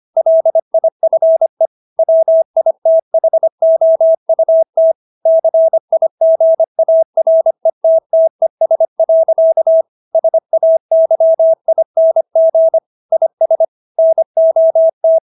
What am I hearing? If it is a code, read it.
Morse code, heard as a keyed tone, about 25 wpm, 650 Hz. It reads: LIFE WITHOUT CIGARETTES. SAYING IS NOT